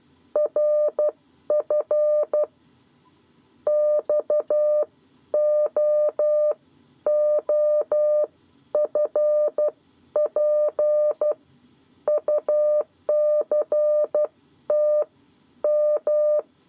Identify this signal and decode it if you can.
Morse code, heard as a keyed tone, about 12 wpm, spaced as for 10 wpm, 600 Hz. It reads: RF XOOFP UCT M
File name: 7 Lezione cw 2